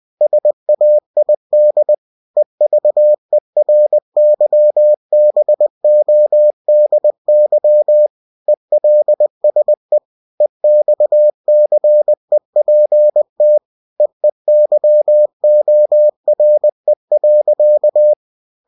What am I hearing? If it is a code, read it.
Morse code, heard as a keyed tone, about 20 wpm, 600 Hz. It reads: SAID EVERYBODY ELSE EXCEPT EEYORE.
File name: cw-books / winnie-the-pooh / winnie-20wpm-page-0109